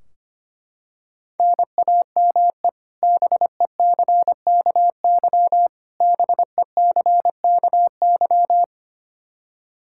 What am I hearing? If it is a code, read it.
Morse code, heard as a keyed tone, about 25 wpm, 700 Hz. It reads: NAME BECKY BECKY
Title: K1USN SST QSO Practice 25wpm